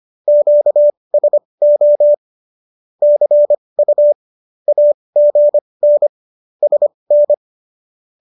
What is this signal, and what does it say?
Morse code, heard as a keyed tone, about 25 wpm, spaced as for 20 wpm, 590 Hz. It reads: QSO CU AGN SN
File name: qso221 20